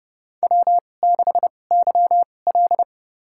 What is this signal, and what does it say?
Morse code, heard as a keyed tone, about 30 wpm, spaced as for 22 wpm, 700 Hz. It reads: W6YL
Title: USA Call Signs - 4 Characters - 2x Character Spacing - 30wpm